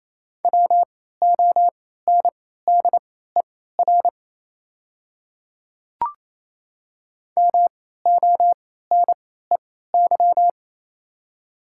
Morse code, heard as a keyed tone, about 28 wpm, spaced as for 16 wpm, 690 Hz. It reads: WONDER MONEY
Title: 3x Character Spacing - Top 500 Words - 28wpm